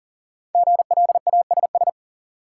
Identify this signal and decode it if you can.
Morse code, heard as a keyed tone, about 40 wpm, 700 Hz. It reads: GLASS
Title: Sentences from Top 1000 Words - Review Part 3 - New Word Drill - 40wpm